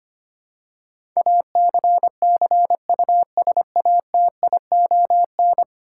Morse code, heard as a keyed tone, about 25 wpm, 700 Hz. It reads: ACCUSATION
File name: 10 Letter Words - First 3 Letters Given - 25wpm